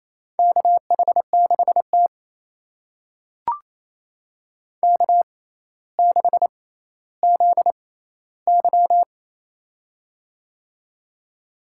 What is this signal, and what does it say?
Morse code, heard as a keyed tone, about 28 wpm, spaced as for 14 wpm, 700 Hz. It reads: KH6T K6ZY
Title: USA Call Signs - 4 Characters - 6x Character Spacing - 28wpm